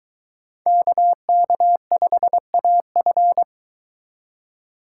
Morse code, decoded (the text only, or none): KK5AF